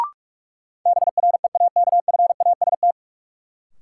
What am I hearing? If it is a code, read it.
Morse code, heard as a keyed tone, about 46 wpm, 700 Hz. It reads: BREAKFAST
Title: Sentences from Top 1500 Words - New Word Drill - 45wpm